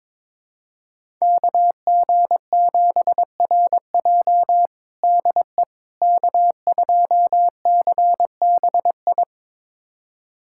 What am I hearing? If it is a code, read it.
Morse code, heard as a keyed tone, about 22 wpm, 700 Hz. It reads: KG7RJ DE K2CBI
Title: USA Callsign DE Callsign - 5 Characters - 22wpm